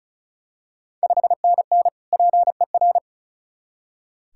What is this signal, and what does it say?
Morse code, heard as a keyed tone, about 36 wpm, 700 Hz. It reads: 5NN PER
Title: Ontario QSO Party Exchange 35wpm